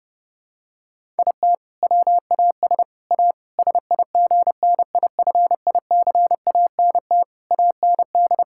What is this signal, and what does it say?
Morse code, heard as a keyed tone, about 32 wpm, 700 Hz. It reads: IT WAS A SIGNIFICANT AND
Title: Sentences from Top 1000 Words - Review Part 6 - 30wpm